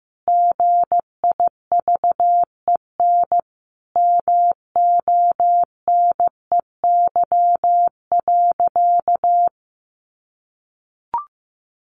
Morse code, heard as a keyed tone, about 15 wpm, 700 Hz. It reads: GIVEN MONEY.
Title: Sentences from Top 1400 Words 15wpm